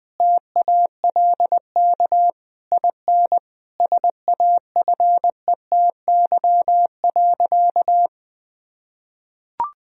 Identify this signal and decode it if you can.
Morse code, heard as a keyed tone, about 20 wpm, 700 Hz. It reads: TALK IN SAFETY.
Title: Sentences from Top 1000 Words - Review Part 4 - 20wpm